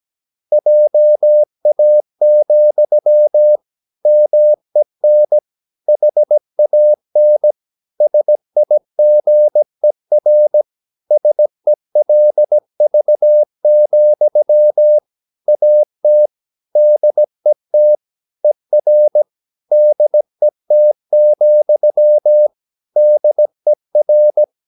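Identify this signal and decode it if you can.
Morse code, heard as a keyed tone, about 17 wpm, 600 Hz. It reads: JA, MEN HAN SIGER SELV, AT DET ER DET, DER
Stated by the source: Slow (QRS) Morse practice at 17 wpm, from CW med Gnister.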